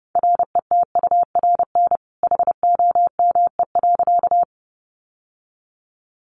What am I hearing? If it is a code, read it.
Morse code, heard as a keyed tone, about 30 wpm, 700 Hz. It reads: RETURN HOME.
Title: Sentences from Top 800 Words 30wpm